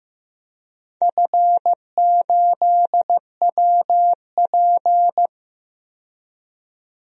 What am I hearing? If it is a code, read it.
Morse code, heard as a keyed tone, about 15 wpm, 700 Hz. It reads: F8WP
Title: DXCC-100 Call Signs - 4 Characters - 15wpm